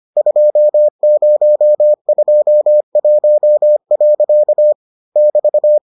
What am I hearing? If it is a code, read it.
Morse code, heard as a keyed tone, about 25 wpm, 600 Hz. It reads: 2021. <BT>